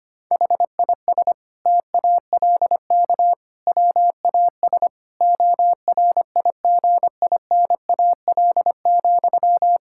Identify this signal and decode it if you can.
Morse code, heard as a keyed tone, about 25 wpm, 700 Hz. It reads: HIS TALK WAS ORIGINAL,
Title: Sentences from Top 1000 Words - Review Part 6 - 25wpm